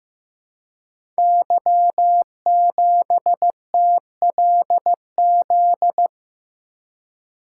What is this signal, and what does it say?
Morse code, heard as a keyed tone, about 15 wpm, 700 Hz. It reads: Y7TLZ
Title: Code Groups - 5 Letter:Numbers - 15wpm